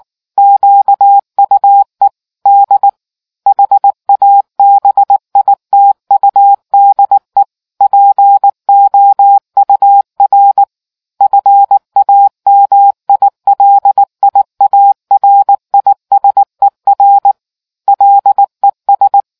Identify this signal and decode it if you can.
Morse code, heard as a keyed tone, about 19 wpm, 800 Hz. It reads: QUE D HABITUDE POUR FAMILIARISER LES